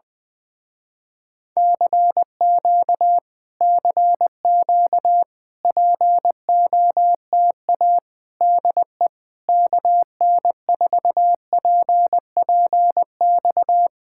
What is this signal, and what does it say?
Morse code, heard as a keyed tone, about 20 wpm, 700 Hz. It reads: CQ CQ POTA DE KN4PPX
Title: Speed-Racing Common QSO Phrases - 17wpm